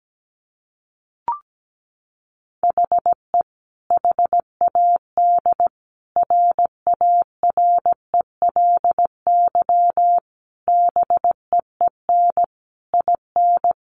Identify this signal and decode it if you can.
Morse code, heard as a keyed tone, about 17 wpm, 700 Hz. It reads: HE HAD RARELY BEEN IN
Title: Sentences from Top 1400 Words 17wpm